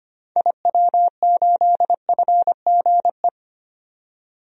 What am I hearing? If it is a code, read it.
Morse code, heard as a keyed tone, about 25 wpm, 700 Hz. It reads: IW8FGE
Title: DXCC-100 Call Signs - 6 Characters - 25wpm